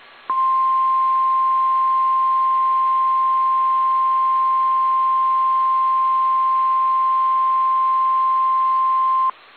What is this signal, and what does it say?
Emergency Alert System (SAME) tones (the 1050 Hz attention tone). It is heard from 0.2 to 9.4 s.
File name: eas / KSDZKDJL / KSDZKDJL